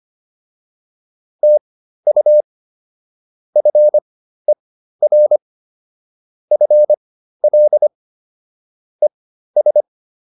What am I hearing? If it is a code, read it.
Morse code, heard as a keyed tone, about 25 wpm, spaced as for 13 wpm, 600 Hz. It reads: TU FER FL ES